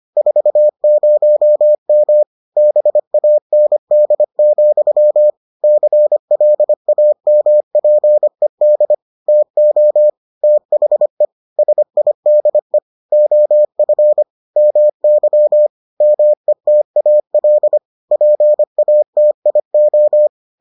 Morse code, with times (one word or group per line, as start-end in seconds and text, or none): 0.2-2.2 40M
2.6-5.3 BAND,
5.6-8.9 CLAMPED
9.3-10.1 TO
10.4-11.3 THE
11.6-12.8 SIDE
13.1-14.2 OF
14.6-15.7 MY
16.0-17.8 METAL
18.1-20.3 PATIO